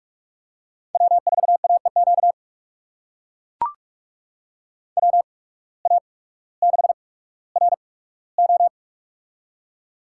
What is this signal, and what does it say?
Morse code, heard as a keyed tone, about 42 wpm, spaced as for 19 wpm, 700 Hz. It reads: W4AEQ WA6RO
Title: USA Call Signs - 5 Characters - 8x Character Spacing - 45wpm